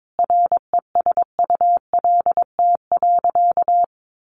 Morse code, heard as a keyed tone, about 22 wpm, 700 Hz. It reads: RESULT.
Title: Sentences from Top 800 Words 22wpm